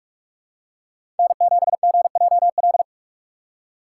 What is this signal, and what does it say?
Morse code, heard as a keyed tone, about 46 wpm, 690 Hz. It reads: N7GJL
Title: USA Call Signs - 5 Characters - 4x Character Spacing - 45wpm